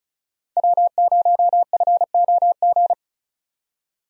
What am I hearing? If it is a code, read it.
Morse code, heard as a keyed tone, about 36 wpm, 700 Hz. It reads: W0FOG
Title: USA Call Signs - 5 Characters - 35wpm